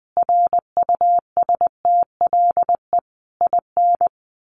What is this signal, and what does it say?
Morse code, heard as a keyed tone, about 19 wpm, 700 Hz. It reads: RUSTLE IN
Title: A Sound Of Thunder 20wpm